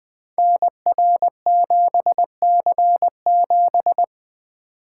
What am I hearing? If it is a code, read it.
Morse code, heard as a keyed tone, about 20 wpm, 700 Hz. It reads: NR7C7
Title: Code Groups - 5 Letter:Numbers - 20wpm